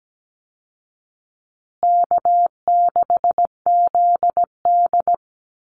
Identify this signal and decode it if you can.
Morse code, heard as a keyed tone, about 17 wpm, 700 Hz. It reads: K6ZD
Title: Call Signs Repeated 2x - 17wpm